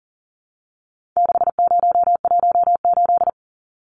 Morse code, heard as a keyed tone, about 40 wpm, 700 Hz. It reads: 6018
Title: Numbers - 4-Digit 40wpm